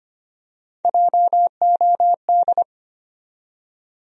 Morse code, heard as a keyed tone, about 25 wpm, 700 Hz. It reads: JOD
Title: USA Call Sign Postfixes - 3 Letters 25wpm